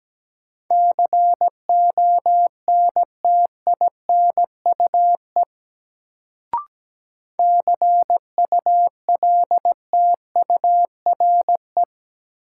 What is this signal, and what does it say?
Morse code, heard as a keyed tone, about 17 wpm, 700 Hz. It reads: CONTINUE CULTURE
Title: Sentences from Top 1000 Words - Review Part 6 - New Word Drill - 17wpm